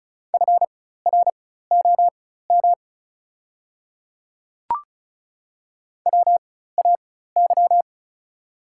Morse code, heard as a keyed tone, about 36 wpm, spaced as for 16 wpm, 700 Hz. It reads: FROM WAY
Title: 4x Character Spacing - Top 100 Words - 35wpm